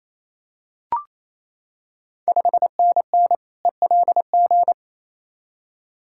Morse code, heard as a keyed tone, about 28 wpm, 690 Hz. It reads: 5NN ELG
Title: Ontario QSO Party Exchange 28wpm